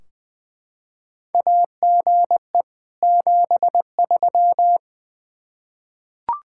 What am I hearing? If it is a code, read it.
Morse code, heard as a keyed tone, about 20 wpm, 700 Hz. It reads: AGE 73
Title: Practice QSO - Exchange 3 - More Info - 20wpm